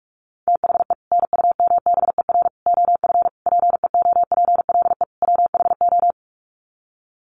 Morse code, heard as a keyed tone, about 44 wpm, 700 Hz. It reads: THE NUMBER OF PEOPLE WHO
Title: Sets of 5 Words Encore 45wpm